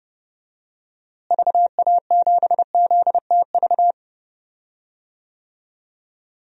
Morse code, heard as a keyed tone, about 30 wpm, 700 Hz. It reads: VA7ZTV
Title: DXCC-100 Call Signs - 6 Characters - Repeated 2x - 30wpm